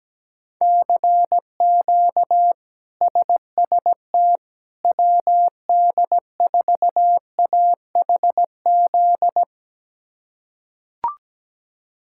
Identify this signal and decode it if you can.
Morse code, heard as a keyed tone, about 17 wpm, 700 Hz. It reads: CQ SST WD4AHZ
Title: K1USN SST QSO Practice - 17wpm